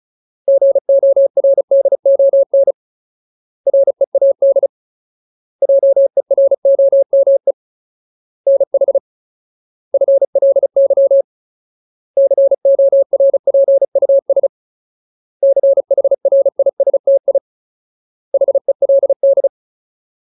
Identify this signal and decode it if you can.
Morse code, heard as a keyed tone, about 35 wpm, 550 Hz. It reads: GORDON READ JEROME NH FLY CORPUS CHRISTI HELD
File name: Ultimate Word List / 35WPM Ultimate